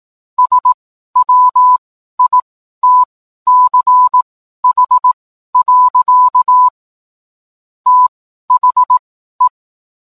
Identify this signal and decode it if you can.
Morse code, heard as a keyed tone, about 18 wpm, spaced as for 12 wpm, 1000 Hz. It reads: SWITCH. THE